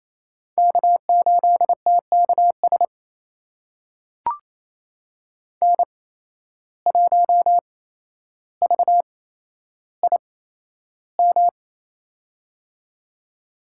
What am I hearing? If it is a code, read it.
Morse code, heard as a keyed tone, about 28 wpm, spaced as for 12 wpm, 700 Hz. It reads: K8TKS N1VIM
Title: USA Call Signs - 5 Characters - 8x Character Spacing - 28wpm